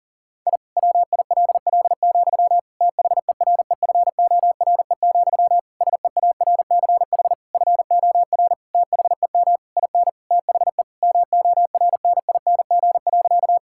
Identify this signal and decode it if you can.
Morse code, heard as a keyed tone, about 40 wpm, 700 Hz. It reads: I WILL, THEREFORE, SEARCH FOR THEM IN THE MORNING.
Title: Sentences from Top 1000 Words - Review Part 5 - 40wpm